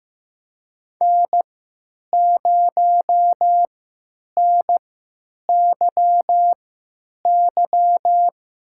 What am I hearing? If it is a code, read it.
Morse code, heard as a keyed tone, about 16 wpm, spaced as for 9 wpm, 700 Hz. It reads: N0NYY
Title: USA Call Signs - 5 Characters - 3x Character Spacing - 15wpm